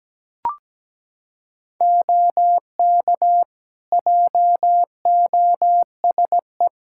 Morse code, heard as a keyed tone, about 17 wpm, 700 Hz. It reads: OK JOSE
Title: Practice QSO - Exchange 3 - More Info - 17wpm